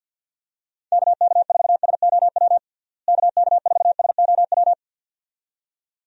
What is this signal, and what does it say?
Morse code, heard as a keyed tone, about 50 wpm, 700 Hz. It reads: KK4SOW KK4SOW
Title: USA Call Signs - 6 Characters - Repeated 2x - 50wpm